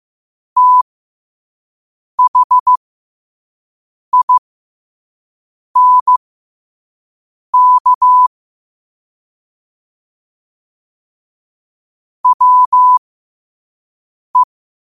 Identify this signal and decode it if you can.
Morse code, heard as a keyed tone, about 15 wpm, spaced as for 5 wpm, 1000 Hz. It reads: THINK WE
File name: qotdfiles / 5wpm